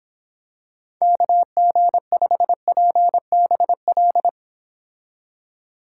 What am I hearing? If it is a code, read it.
Morse code, heard as a keyed tone, about 26 wpm, 700 Hz. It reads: KG5PBL